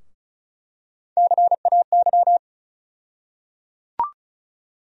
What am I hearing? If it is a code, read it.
Morse code, heard as a keyed tone, about 35 wpm, 700 Hz. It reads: CAY
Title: USA Call Sign Postfixes - 3 Letters 35wpm